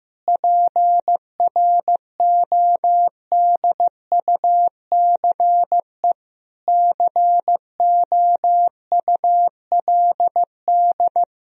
Morse code, heard as a keyed tone, about 15 wpm, 700 Hz. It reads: PRODUCE COULD
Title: Sentences from Top 1000 Words - Review Part 3 - 15wpm